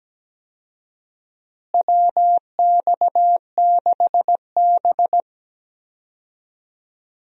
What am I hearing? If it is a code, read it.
Morse code, heard as a keyed tone, about 17 wpm, 700 Hz. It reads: WX6B
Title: USA Call Signs - 4 Characters - Repeated 2x - 17wpm